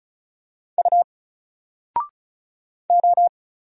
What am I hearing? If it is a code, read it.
Morse code, heard as a keyed tone, about 34 wpm, 700 Hz. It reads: U O
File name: Single-Letter-Number-35wpm